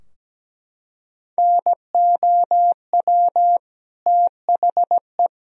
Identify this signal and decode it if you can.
Morse code, heard as a keyed tone, about 17 wpm, 700 Hz. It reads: NOW THE
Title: Sentences from Top 1000 Words - Review Part 3 - 17wpm